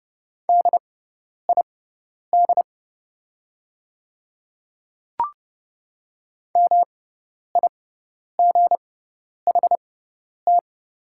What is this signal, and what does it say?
Morse code, heard as a keyed tone, about 30 wpm, spaced as for 10 wpm, 690 Hz. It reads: DID MIGHT
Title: 6x Character Spacing - Top 500 Words - 30wpm